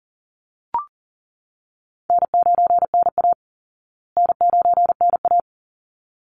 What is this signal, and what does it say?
Morse code, heard as a keyed tone, about 37 wpm, 700 Hz. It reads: N9NA N9NA
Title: Easy Call Signs Repeated 2x - 40wpm